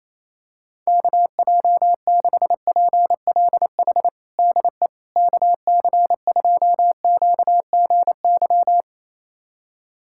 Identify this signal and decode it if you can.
Morse code, heard as a keyed tone, about 28 wpm, 700 Hz. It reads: KJ6PLH DE KC2QGY
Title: USA Callsign DE Callsign - 6 Characters - 28wpm